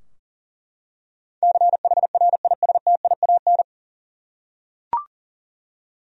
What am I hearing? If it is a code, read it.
Morse code, heard as a keyed tone, about 38 wpm, 700 Hz. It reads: CHRISTIAN
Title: USA Common Names - 40wpm